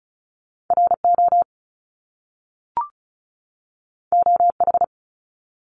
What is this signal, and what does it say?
Morse code, heard as a keyed tone, about 36 wpm, 700 Hz. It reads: RO OH
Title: Top Two Letter Combinations in Words 35wpm